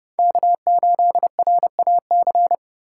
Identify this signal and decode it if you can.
Morse code, heard as a keyed tone, about 36 wpm, 700 Hz. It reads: K8RAC
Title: Speed Racing - USA 1x3 Call Signs 30wpm